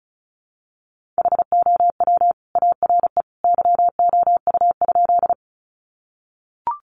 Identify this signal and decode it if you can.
Morse code, heard as a keyed tone, about 34 wpm, 700 Hz. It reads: HOW ARE YOU?
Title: Sentences from Top 100 Words 35wpm